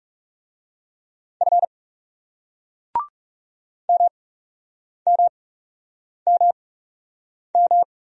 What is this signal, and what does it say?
Morse code, heard as a keyed tone, about 39 wpm, spaced as for 8 wpm, 700 Hz. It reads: F MMMM